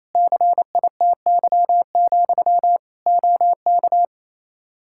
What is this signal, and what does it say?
Morse code, heard as a keyed tone, about 28 wpm, 700 Hz. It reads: CITY, OK
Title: Practice QSO - Exchange 1 - Beginning - 28wpm